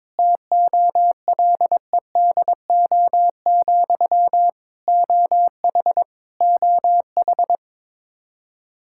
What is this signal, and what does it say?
Morse code, heard as a keyed tone, about 22 wpm, 700 Hz. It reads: TOLEDO, OH OH
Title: Practice QSO - Exchange 1 - Beginning - 22wpm